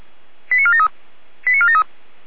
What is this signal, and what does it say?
Five-tone selective call (ZVEI), heard as quick runs of five tones, tones of about 70 ms. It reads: 89473 89473
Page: /455/SDR/ZVEI/